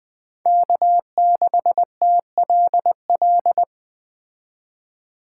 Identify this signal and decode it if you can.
Morse code, heard as a keyed tone, about 20 wpm, 700 Hz. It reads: K6TLL
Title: Speed Racing - USA Call Signs 17wpm